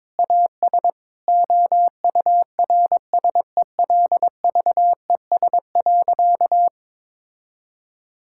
Morse code, heard as a keyed tone, about 22 wpm, 700 Hz. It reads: AS OURSELVES.
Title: Sentences from Top 1600 Words 22wpm